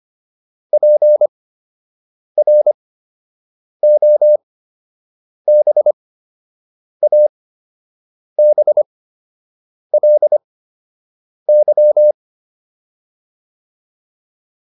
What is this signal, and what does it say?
Morse code, heard as a keyed tone, about 25 wpm, spaced as for 7 wpm, 600 Hz. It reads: PROBABLY